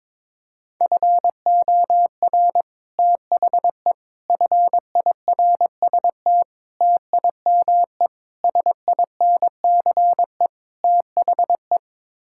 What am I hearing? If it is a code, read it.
Morse code, heard as a keyed tone, about 22 wpm, 700 Hz. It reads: FOR THE FIRST TIME SINCE THE
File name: practice / 22wpm / Sets of 6 Words - 22wpm